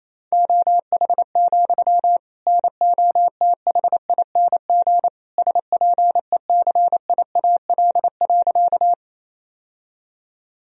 Morse code, heard as a keyed tone, about 28 wpm, 700 Hz. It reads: OH, NOTHING SPECIAL.
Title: Sentences from Top 1000 Words - Review Part 3 - 28wpm